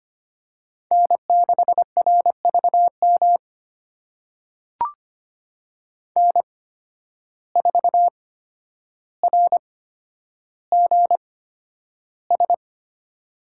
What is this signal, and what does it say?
Morse code, heard as a keyed tone, about 25 wpm, spaced as for 11 wpm, 700 Hz. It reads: N6RVM N4RGS
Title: USA Call Signs - 5 Characters - 8x Character Spacing - 25wpm